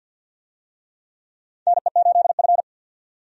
Morse code, heard as a keyed tone, about 49 wpm, 700 Hz. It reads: NE8F
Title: USA Call Signs - 4 Characters - 50wpm